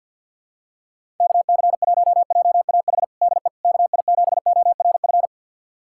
Morse code, heard as a keyed tone, about 52 wpm, 700 Hz. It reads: KC1JAH DE KI7ORF
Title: USA Callsign DE Callsign 50wpm